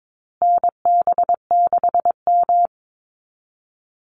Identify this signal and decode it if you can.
Morse code, heard as a keyed tone, about 22 wpm, 700 Hz. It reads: NB6M
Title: Call Signs Repeated 2x - 22wpm